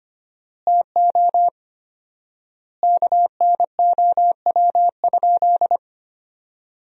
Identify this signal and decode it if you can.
Morse code, heard as a keyed tone, about 25 wpm, 700 Hz. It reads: TO KNOW?
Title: 4x Word-Spacing - Sentences from Top 200 Words - 25wpm